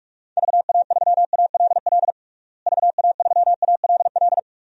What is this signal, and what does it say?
Morse code, heard as a keyed tone, about 45 wpm, 700 Hz. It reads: VA3ALL VA3ALL
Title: DXCC-100 Call Signs - 6 Characters - Repeated 2x - 45wpm